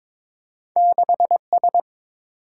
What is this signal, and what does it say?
Morse code, heard as a keyed tone, about 30 wpm, 700 Hz. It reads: 6S